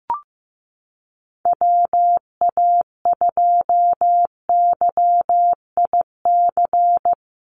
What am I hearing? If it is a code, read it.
Morse code, heard as a keyed tone, about 15 wpm, 700 Hz. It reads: WA2YIC
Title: USA Call Signs Repeated 3x 15wpm